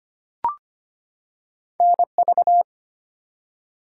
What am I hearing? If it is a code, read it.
Morse code, heard as a keyed tone, about 25 wpm, 700 Hz. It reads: NV
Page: morse-code-ninja-practice-25wpm